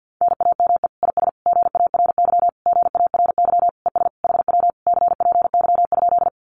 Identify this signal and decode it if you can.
Morse code, heard as a keyed tone, about 50 wpm, 700 Hz. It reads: NAME IS GARY GARY ES HW CPY?
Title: Common QSO Phrases Encore 50wpm